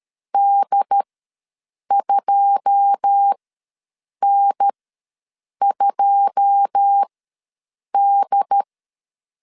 Morse code, heard as a keyed tone, about 13 wpm, spaced as for 7 wpm, 800 Hz. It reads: D2N2D
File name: LEZIONI / 9 2 ws 12wpm 3406